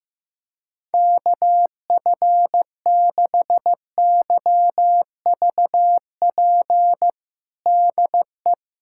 Morse code, heard as a keyed tone, about 15 wpm, 700 Hz. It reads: KF6YVP DE
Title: USA Callsign DE Callsign 15wpm